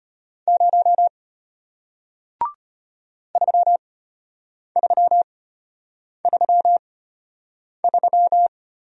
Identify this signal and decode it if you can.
Morse code, heard as a keyed tone, about 33 wpm, spaced as for 8 wpm, 700 Hz. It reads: 0 3333